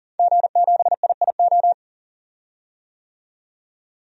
Morse code, heard as a keyed tone, about 40 wpm, 700 Hz. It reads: G7IIO
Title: DXCC-100 Call Signs - 5 Characters - 40wpm